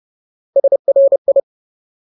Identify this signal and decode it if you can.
Morse code, heard as a keyed tone, about 30 wpm, 550 Hz. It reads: SRI
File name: Ultimate Word List / 30WPM Ultimate